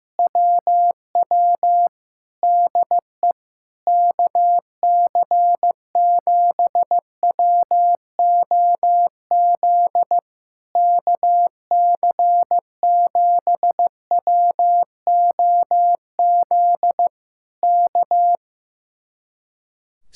Morse code, heard as a keyed tone, about 15 wpm, 700 Hz. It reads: WW DE KC7WOZ KC7WOZ K
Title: Speed-Racing Common QSO Phrases Encore - 15wpm